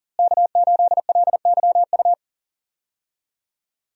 Morse code, heard as a keyed tone, about 38 wpm, 700 Hz. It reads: K8LYU
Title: USA Call Signs - 5 Characters - Repeated 3x - 40wpm